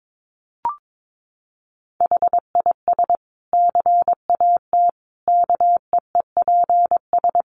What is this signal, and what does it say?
Morse code, heard as a keyed tone, about 22 wpm, 700 Hz. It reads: HIS CAT KEEPS
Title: Additional Easy Phrases 22wpm